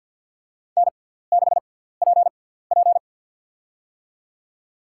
Morse code, heard as a keyed tone, about 53 wpm, spaced as for 17 wpm, 700 Hz. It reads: N6PP
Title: USA Call Signs - 4 Characters - 6x Character Spacing - 50wpm